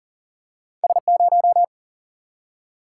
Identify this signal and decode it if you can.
Morse code, heard as a keyed tone, about 38 wpm, 700 Hz. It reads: S0
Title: Single-Letter Single-Number 40wpm